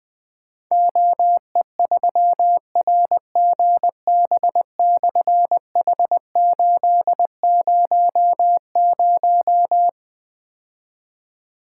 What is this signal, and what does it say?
Morse code, heard as a keyed tone, about 20 wpm, 700 Hz. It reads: OE3RGB/H800
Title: Difficult Call Signs 20wpm